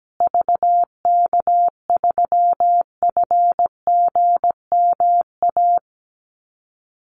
Morse code, heard as a keyed tone, about 17 wpm, 700 Hz. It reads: VK3FGMA
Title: USA Call Signs Repeated 2x 17wpm